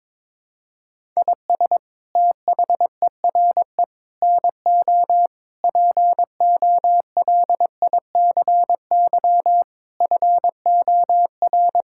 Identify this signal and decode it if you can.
Morse code, heard as a keyed tone, about 22 wpm, 700 Hz. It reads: IS THERE NO POLICY FOR
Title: Sentences from Top 1000 Words - Review Part 5 - 22wpm